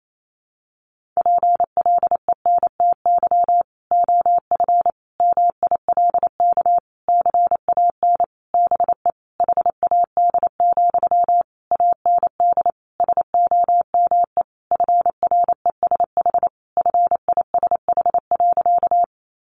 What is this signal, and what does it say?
Morse code, heard as a keyed tone, about 28 wpm, 700 Hz. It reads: PLENTY OF MILK CAN BE HAD, AND SOME FRESH FISH.
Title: Sentences from Top 1500 Words 28wpm